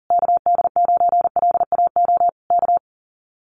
Morse code, heard as a keyed tone, about 40 wpm, 700 Hz. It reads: KD9LAO K
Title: Common QSO Phrases Encore 40wpm